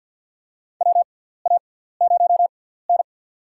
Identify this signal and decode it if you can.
Morse code, heard as a keyed tone, about 47 wpm, spaced as for 17 wpm, 700 Hz. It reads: WA0N
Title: USA Call Signs - 4 Characters - 6x Character Spacing - 50wpm